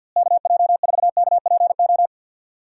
Morse code, heard as a keyed tone, about 51 wpm, 700 Hz. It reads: KJ4KPO
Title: USA Call Signs - 6 Characters - Repeated 2x - 50wpm